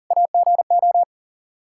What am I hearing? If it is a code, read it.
Morse code, heard as a keyed tone, about 40 wpm, 700 Hz. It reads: AGO
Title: K1USN SST QSO Practice 40wpm